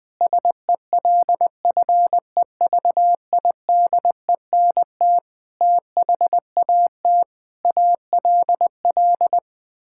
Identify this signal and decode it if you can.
Morse code, heard as a keyed tone, about 20 wpm, 700 Hz. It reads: SELFEVIDENT THAT ALL
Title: Sets of 9 Words - 20wpm